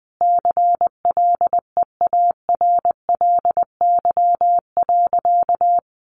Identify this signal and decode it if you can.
Morse code, heard as a keyed tone, about 20 wpm, 700 Hz. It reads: CLEARLY.
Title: Sentences from Top 1300 Words 20wpm